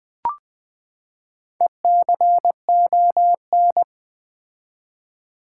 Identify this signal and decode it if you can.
Morse code, heard as a keyed tone, about 20 wpm, 700 Hz. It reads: ECON
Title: Build A Word - 4 Letters At A Time - 8-Letter Words - 20wpm